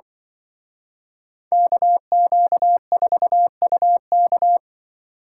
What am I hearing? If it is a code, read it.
Morse code, heard as a keyed tone, about 25 wpm, 700 Hz. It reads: KQ4UK
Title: Speed Racing - USA 2x2 Call Signs 20wpm